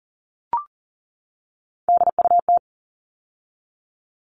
Morse code, heard as a keyed tone, about 40 wpm, 700 Hz. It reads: DUT
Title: All Cut Numbers - 3-Digit 40wpm